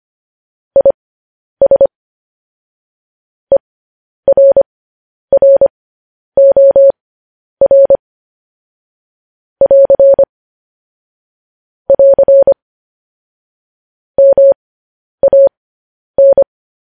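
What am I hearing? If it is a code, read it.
Morse code, heard as a keyed tone, about 25 wpm, spaced as for 10 wpm, 550 Hz. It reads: IS ERROR <AR> <AR> MAN
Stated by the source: Morse code news headlines sent at 10 wpm.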